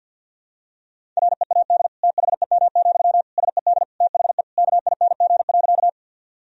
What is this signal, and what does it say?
Morse code, heard as a keyed tone, about 52 wpm, 700 Hz. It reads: READ THEM, SED THE KING.
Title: Sentences from Top 400 Words and Abbreviations 50wpm